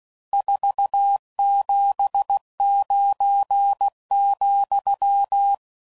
Morse code, heard as a keyed tone, about 16 wpm, 800 Hz. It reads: 479,
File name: CW Code Groups @15WPM / RSTs / Various RST Nos